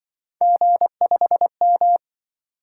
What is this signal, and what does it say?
Morse code, heard as a keyed tone, about 25 wpm, 700 Hz. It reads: G5M